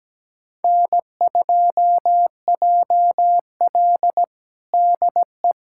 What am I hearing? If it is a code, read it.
Morse code, heard as a keyed tone, about 17 wpm, 700 Hz. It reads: N2JL DE
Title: USA Callsign DE Callsign - 4 Characters - 17wpm